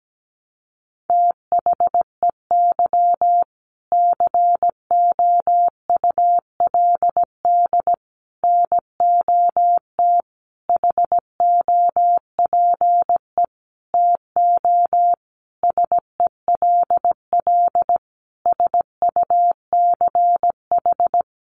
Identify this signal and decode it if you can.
Morse code, heard as a keyed tone, about 17 wpm, 700 Hz. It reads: THEY COULD NOT HOPE TO SELL SUCH
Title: Sentences from Top 1100 Words 17wpm